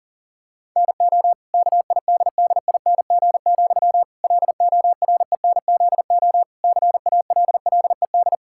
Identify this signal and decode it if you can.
Morse code, heard as a keyed tone, about 40 wpm, 700 Hz. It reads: NO KIDDING, LORENZO CALLED
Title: Pangram Challenge v2 - 40wpm